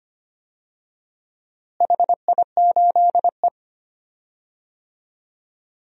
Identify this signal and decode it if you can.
Morse code, heard as a keyed tone, about 25 wpm, 700 Hz. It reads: HI8E
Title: DXCC-100 Call Signs - 4 Characters - Repeated 2x - 25wpm